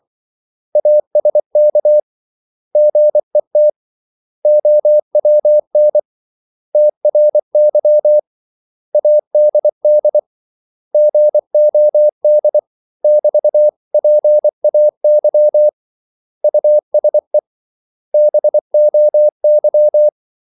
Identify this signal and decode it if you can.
Morse code, heard as a keyed tone, about 24 wpm, 600 Hz. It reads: ASK GET OWN TRY ADD GOD <BT>PAY USE BOY
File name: CW / Practice / 500MostCommonEnglishWords24